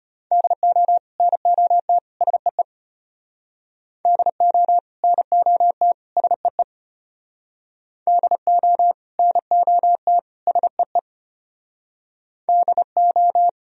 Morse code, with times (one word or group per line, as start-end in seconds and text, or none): 0.3-1.0 DO
1.2-2.0 NOT
2.2-2.6 SEE
4.0-4.8 DO
5.0-5.9 NOT
6.2-6.6 SEE
8.1-8.9 DO
9.2-10.2 NOT
10.5-11.0 SEE
12.5-13.5 DO